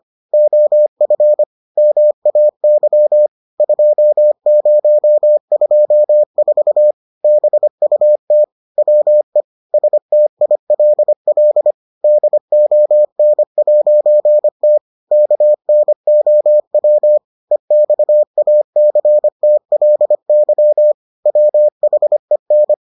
Morse code, heard as a keyed tone, about 25 wpm, 600 Hz. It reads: OF MAY 2024 BUT WE STILL DON'T KNOW EXACTLY WHEN